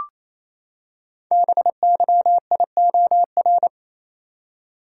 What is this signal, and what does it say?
Morse code, heard as a keyed tone, about 28 wpm, 700 Hz. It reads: BYIOR
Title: Code Groups - 5 Letters - 28wpm